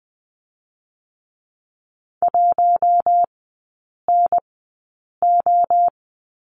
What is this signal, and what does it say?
Morse code, heard as a keyed tone, about 20 wpm, spaced as for 8 wpm, 700 Hz. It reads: 1NO
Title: 060 Lesson 11e - Letter h - Three Characters 20wpm